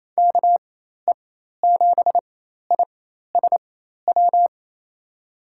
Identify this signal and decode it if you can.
Morse code, heard as a keyed tone, about 28 wpm, spaced as for 13 wpm, 700 Hz. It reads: KE7ISW
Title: USA Call Signs - 6 Characters - 4x Character Spacing - 28wpm